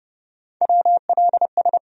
Morse code, heard as a keyed tone, about 30 wpm, 700 Hz. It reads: WLS
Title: USA Call Sign Postfixes - 3 Letters 30wpm